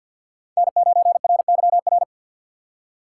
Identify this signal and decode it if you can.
Morse code, heard as a keyed tone, about 47 wpm, 700 Hz. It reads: N9RYR
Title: USA Call Signs - 5 Characters - 2x Character Spacing - 50wpm